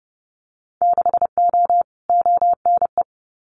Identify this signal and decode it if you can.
Morse code, heard as a keyed tone, about 30 wpm, 700 Hz. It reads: 6O ONE
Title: Winter Field Day Exchanges Encore 30wpm